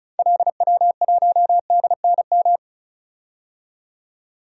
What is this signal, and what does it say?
Morse code, heard as a keyed tone, about 36 wpm, 700 Hz. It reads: LW1DNM
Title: DXCC-100 Call Signs - 6 Characters - 35wpm